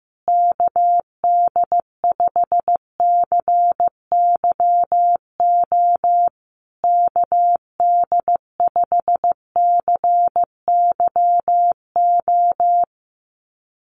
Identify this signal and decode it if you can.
Morse code, heard as a keyed tone, about 15 wpm, 700 Hz. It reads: KD5CYO KD5CYO
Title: USA Call Signs Repeated 3x 15wpm